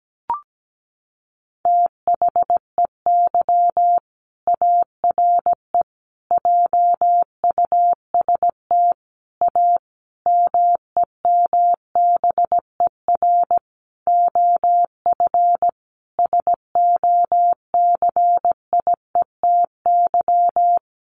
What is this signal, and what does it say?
Morse code, heard as a keyed tone, about 17 wpm, 700 Hz. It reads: THEY ARE JUST A MEMBER OF SOCIETY
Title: Sentences from Top 1100 Words 17wpm